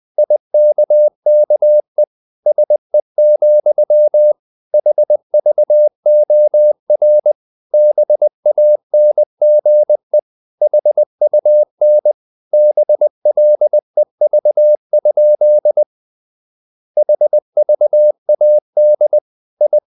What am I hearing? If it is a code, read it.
Morse code, heard as a keyed tone, about 20 wpm, 600 Hz. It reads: IKKE SE, HVOR BANGE HUN BLEV? HVAD I
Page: Ferie 20 wpm